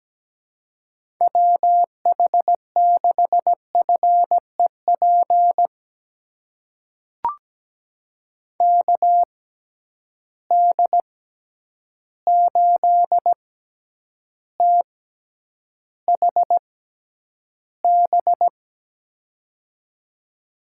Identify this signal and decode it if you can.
Morse code, heard as a keyed tone, about 17 wpm, spaced as for 9 wpm, 700 Hz. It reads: WH6FEP KD8THB
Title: USA Call Signs - 6 Characters - 6x Character Spacing - 17wpm